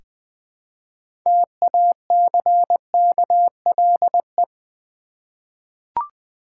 Morse code, heard as a keyed tone, about 20 wpm, 700 Hz. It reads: TACKLE
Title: 6 Letter Words - First 3 Letters Given - 20wpm